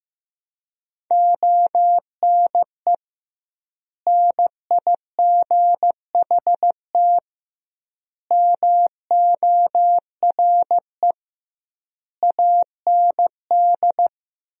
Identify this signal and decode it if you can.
Morse code, heard as a keyed tone, about 15 wpm, 700 Hz. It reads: ONE NIGHT MORE AND
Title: 2x Word-Spacing - Sentences from Top 300 Words - 15wpm